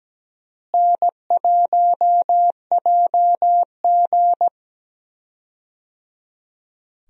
Morse code, heard as a keyed tone, about 17 wpm, 700 Hz. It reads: N1JG
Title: USA Call Signs - 4 Characters - Repeated 3x - 17wpm